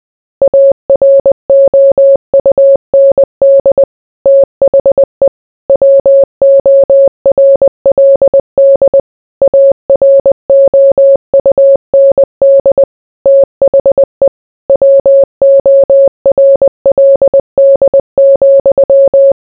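Morse code, heard as a keyed tone, about 20 wpm, 550 Hz. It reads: AROUND THE WORLD AROUND THE WORLD,
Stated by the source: Morse code mixed with noise and music as background sound for sleep.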